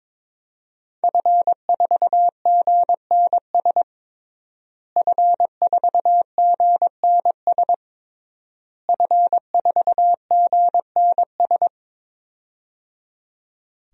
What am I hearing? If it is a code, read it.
Morse code, heard as a keyed tone, about 22 wpm, 700 Hz. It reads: F4GNS F4GNS F4GNS
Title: DXCC-100 Call Signs - 5 Characters - Repeated 3x - 22wpm